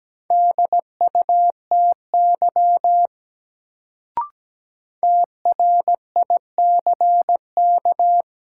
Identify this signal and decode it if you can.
Morse code, heard as a keyed tone, about 17 wpm, 700 Hz. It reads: DUTY TRICK
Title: Sentences from Top 1500 Words - New Word Drill - 17wpm